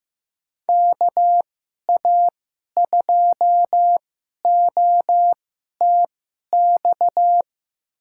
Morse code, heard as a keyed tone, about 15 wpm, spaced as for 11 wpm, 700 Hz. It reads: KA2OTX
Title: USA Call Signs - 2x Character Spacing - 15wpm